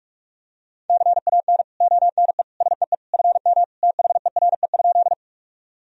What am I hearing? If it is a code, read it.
Morse code, heard as a keyed tone, about 46 wpm, 700 Hz. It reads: CAN ONE SEE FM THERE?
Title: Sentences from Top 100 Words and Abbreviations 45wpm